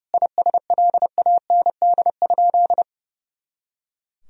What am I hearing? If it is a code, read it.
Morse code, heard as a keyed tone, about 30 wpm, 700 Hz. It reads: ISLAND?
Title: Sentences from Top 500 Words and Abbreviations 30wpm